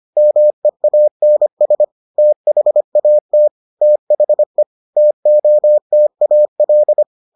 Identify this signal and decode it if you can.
Morse code, heard as a keyed tone, about 25 wpm, 600 Hz. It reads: MEANS THAT THE TOTAL